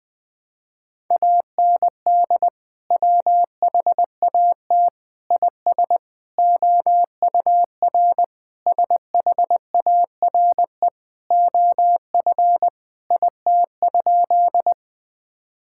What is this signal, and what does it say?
Morse code, heard as a keyed tone, about 20 wpm, 700 Hz. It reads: AND WHAT IS OUR SHARE OF IT?
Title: Sentences from Top 1000 Words - Review Part 6 - 20wpm